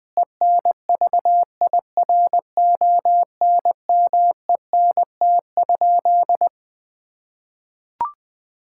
Morse code, heard as a keyed tone, about 20 wpm, 700 Hz. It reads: ENVIRONMENT?
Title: Sentences from Top 1000 Words - Review Part 5 - 20wpm